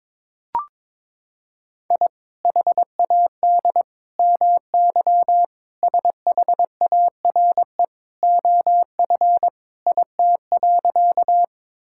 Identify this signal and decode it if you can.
Morse code, heard as a keyed tone, about 22 wpm, 700 Hz. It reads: I HAD MY SHARE OF IT.
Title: Sentences from Top 1000 Words - Review Part 6 - 22wpm